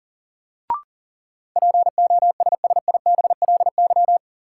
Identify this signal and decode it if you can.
Morse code, heard as a keyed tone, about 40 wpm, 700 Hz. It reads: POSSIBLY
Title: Sentences from Top 1300 Words - New Word Drill Alt - 40wpm